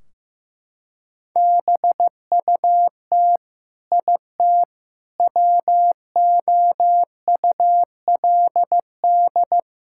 Morse code, heard as a keyed tone, about 15 wpm, 700 Hz. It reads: BUT IT WOULD
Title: Sentences from Top 1600 Words 15wpm